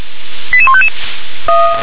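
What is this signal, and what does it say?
Five-tone selective call (ZVEI), heard as quick runs of five tones, tones of about 70 ms. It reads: 88155
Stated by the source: Five-tone ZVEI selective call received by SDR, code 88152.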